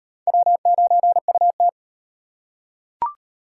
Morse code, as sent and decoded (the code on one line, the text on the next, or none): .-- ----. ..- -
W9UT